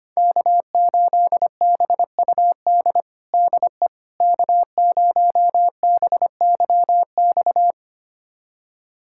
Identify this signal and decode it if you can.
Morse code, heard as a keyed tone, about 25 wpm, 700 Hz. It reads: K8BUD DE K0BYX
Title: USA Callsign DE Callsign - 5 Characters - 25wpm